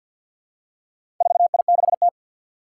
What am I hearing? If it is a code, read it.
Morse code, heard as a keyed tone, about 52 wpm, 700 Hz. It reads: 4I6T
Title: Four Challenging Letter-Number Combinations 50wpm